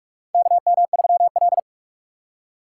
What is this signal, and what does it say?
Morse code, heard as a keyed tone, about 46 wpm, 700 Hz. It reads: KM3L
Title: USA Call Signs - 4 Characters - 2x Character Spacing - 45wpm